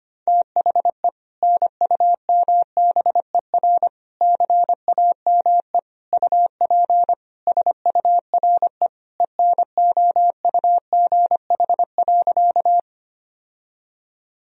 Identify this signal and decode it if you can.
Morse code, heard as a keyed tone, about 25 wpm, 700 Hz. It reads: THE NUMBER CAME UP SURE ENOUGH.
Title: Sentences from Top 1000 Words - Review Part 1 - 25wpm